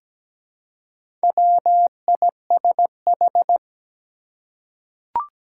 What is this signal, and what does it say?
Morse code, heard as a keyed tone, about 17 wpm, 700 Hz. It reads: WISH
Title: Four Letter Words - 17wpm